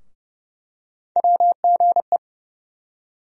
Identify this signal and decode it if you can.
Morse code, heard as a keyed tone, about 30 wpm, 700 Hz. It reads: WGE
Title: USA Call Sign Postfixes - 3 Letters 30wpm